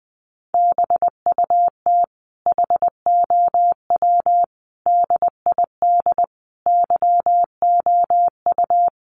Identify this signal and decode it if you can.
Morse code, heard as a keyed tone, about 20 wpm, 700 Hz. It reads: BUT HOW DID YOU
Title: Sentences from Top 1400 Words 20wpm